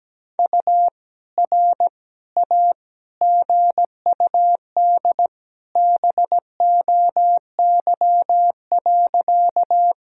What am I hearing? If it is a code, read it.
Morse code, heard as a keyed tone, about 17 wpm, 700 Hz. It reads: U R A GUD BOY.
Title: Sentences from Top 200 Words and Abbreviations 17wpm